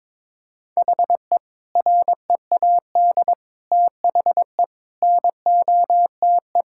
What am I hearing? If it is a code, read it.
Morse code, heard as a keyed tone, about 22 wpm, 700 Hz. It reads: HE READ THE NOTE
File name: practice / 22wpm / Sentences with 4-Word Phrases - 22wpm